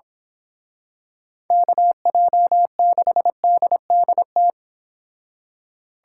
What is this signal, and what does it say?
Morse code, heard as a keyed tone, about 26 wpm, 700 Hz. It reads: KJ6DDT
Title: Speed Racing - USA 2x3 Call Signs 22wpm